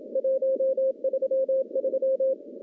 Morse code, heard as a keyed tone, about 30 wpm, 500 Hz. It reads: 133
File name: contests / CW-SS-k5tr- -0900